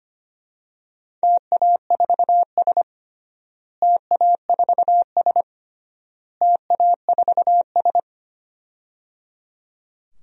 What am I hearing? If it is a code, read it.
Morse code, heard as a keyed tone, about 25 wpm, 700 Hz. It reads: TA4S TA4S TA4S
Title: DXCC-100 Call Signs - 4 Characters - Repeated 3x - 25wpm